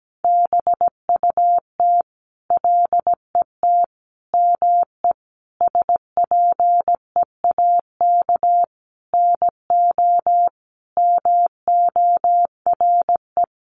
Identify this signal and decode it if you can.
Morse code, heard as a keyed tone, about 17 wpm, 700 Hz. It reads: BUT LET ME SPEAK NO MORE
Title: Sentences from Top 1100 Words 17wpm